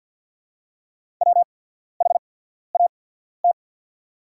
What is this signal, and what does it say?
Morse code, heard as a keyed tone, about 48 wpm, spaced as for 14 wpm, 700 Hz. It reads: WHAT
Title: 8x Character Spacing - Top 100 Words - 50wpm